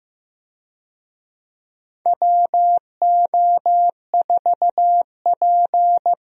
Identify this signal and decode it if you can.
Morse code, heard as a keyed tone, about 16 wpm, 700 Hz. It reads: WO4P
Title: USA Call Signs - 4 Characters - Repeated 3x - 15wpm